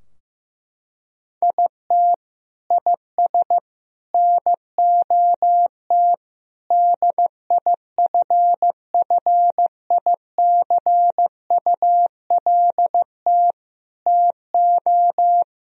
Morse code, heard as a keyed tone, about 15 wpm, 700 Hz. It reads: IT IS NOT DIFFICULT TO
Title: Sentences from Top 1000 Words - Review Part 4 - 15wpm